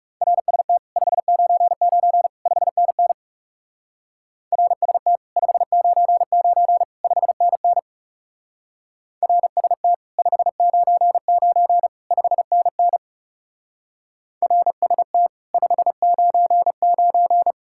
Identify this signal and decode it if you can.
Morse code, heard as a keyed tone, about 38 wpm, 700 Hz. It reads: RST 599 5NN RST 599 5NN RST 599 5NN RST 599